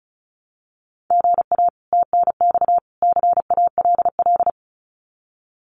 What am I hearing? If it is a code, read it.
Morse code, heard as a keyed tone, about 34 wpm, 700 Hz. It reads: GA TNX CALL
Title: Common QSO Phrases 35wpm